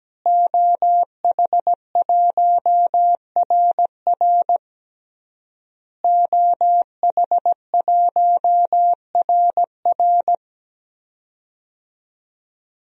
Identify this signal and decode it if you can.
Morse code, heard as a keyed tone, about 17 wpm, 700 Hz. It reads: OH1RR OH1RR
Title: DXCC-100 Call Signs - 5 Characters - Repeated 3x - 17wpm